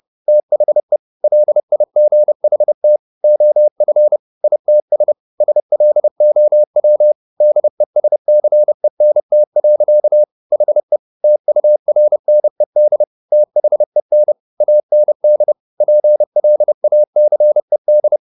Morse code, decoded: THE LIGHT OF ITS SLOW DESCENT. HE TURNED THEN AND PLACED

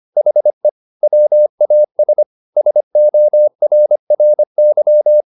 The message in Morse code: HE WAS SORRY

